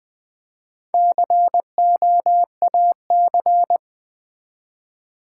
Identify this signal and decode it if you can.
Morse code, heard as a keyed tone, about 20 wpm, 700 Hz. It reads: COAC
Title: Build A Word - 1 Letter At A Time - 5-Letter Words - 20wpm